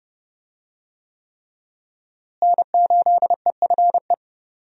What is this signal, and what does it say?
Morse code, heard as a keyed tone, about 28 wpm, 700 Hz. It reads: N8EFE
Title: USA Call Signs - 5 Characters - Repeated 3x - 30wpm